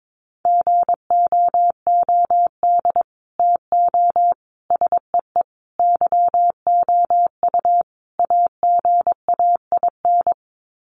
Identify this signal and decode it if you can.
Morse code, heard as a keyed tone, about 22 wpm, 700 Hz. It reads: GOOD TO SEE YOU AGAIN
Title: Sets of 5 Words Encore2 22wpm